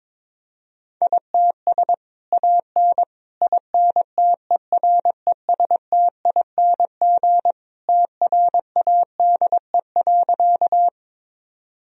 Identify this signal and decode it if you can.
Morse code, heard as a keyed tone, about 22 wpm, 700 Hz. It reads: ITS AN INTERESTING TRADE.
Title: Sentences from Top 1000 Words - Review Part 4 - 22wpm